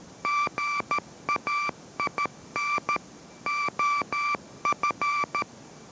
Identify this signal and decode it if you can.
Morse code, heard as a keyed tone, about 13 wpm, 1170 Hz, with noise. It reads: GAIN OF